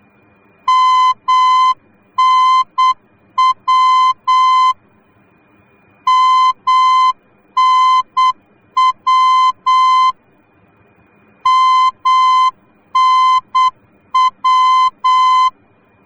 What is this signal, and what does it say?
Morse code, heard as a keyed tone, about 8 wpm, 1030 Hz. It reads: MNW MNW MNW
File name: germany / mx / MNW-D